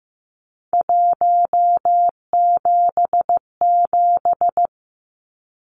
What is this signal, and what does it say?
Morse code, heard as a keyed tone, about 15 wpm, 700 Hz. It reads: 177
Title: Numbers - 3-Digit 15wpm